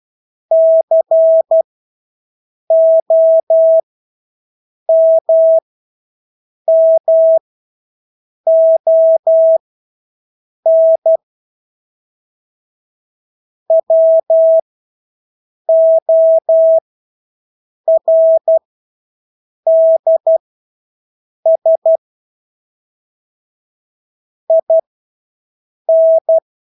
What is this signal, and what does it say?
Morse code, heard as a keyed tone, about 12 wpm, spaced as for 6 wpm, 650 Hz. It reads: COMMON WORDS IN
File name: Engelske ord 12-6 wpm